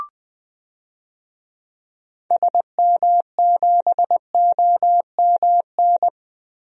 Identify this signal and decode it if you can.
Morse code, heard as a keyed tone, about 20 wpm, 700 Hz. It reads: SM7OMN
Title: DXCC-100 Call Signs - 6 Characters - Repeated 3x - 20wpm